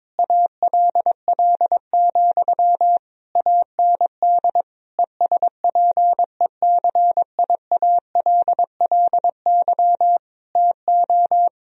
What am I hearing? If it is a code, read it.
Morse code, heard as a keyed tone, about 22 wpm, 700 Hz. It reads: ALL, AND ESPECIALLY TO
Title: Sentences from Top 1000 Words - Review Part 4 - 22wpm